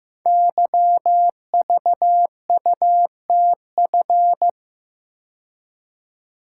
Morse code, heard as a keyed tone, about 15 wpm, 700 Hz. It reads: YVUTF
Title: Code Groups - 5 Letter:Numbers - 15wpm